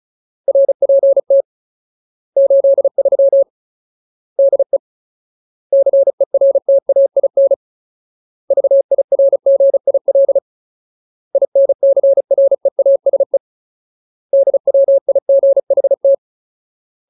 Morse code, heard as a keyed tone, about 34 wpm, 550 Hz. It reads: RPT 83 DE CERTAIN VIRGIL INCREASE DWIGHT
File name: Ultimate Word List / 35WPM Ultimate